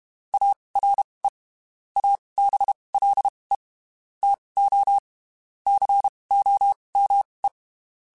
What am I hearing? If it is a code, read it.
Morse code, heard as a keyed tone, about 32 wpm, spaced as for 23 wpm, 800 Hz. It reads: ARE ABLE TO COME
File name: (spaced) / QSOs Spaced-mp3s / WA7QOX 1 @30 sp